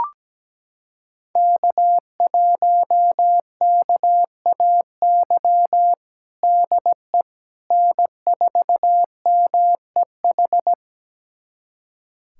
Morse code, heard as a keyed tone, about 17 wpm, 700 Hz. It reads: K1KAY DE N4MEH
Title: USA Callsign DE Callsign - 5 Characters - 17wpm